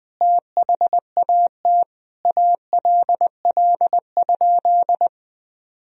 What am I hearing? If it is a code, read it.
Morse code, heard as a keyed tone, about 21 wpm, 700 Hz. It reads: THAT ALL?
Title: Sentences from Top 200 Words and Abbreviations 20wpm